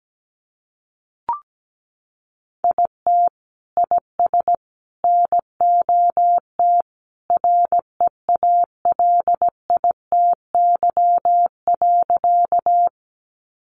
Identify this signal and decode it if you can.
Morse code, heard as a keyed tone, about 17 wpm, 700 Hz. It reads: IT IS NOT REALITY.
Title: Sentences from Top 1200 Words 17wpm